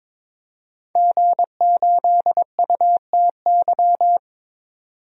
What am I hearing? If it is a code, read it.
Morse code, heard as a keyed tone, about 21 wpm, 700 Hz. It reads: G8UTY
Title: DXCC-100 Call Signs - 5 Characters - 22wpm